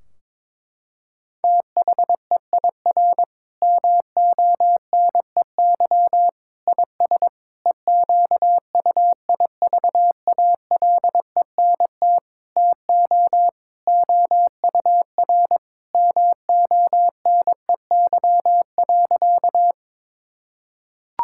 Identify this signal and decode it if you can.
Morse code, heard as a keyed tone, about 22 wpm, 700 Hz. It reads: THEIR MONEY IS EQUIVALENT TO OUR MONEY.
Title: Sentences from Top 1600 Words 22wpm